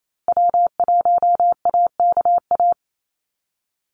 Morse code, heard as a keyed tone, about 28 wpm, 700 Hz. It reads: W1AKA
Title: USA Call Signs Repeated 3x 28wpm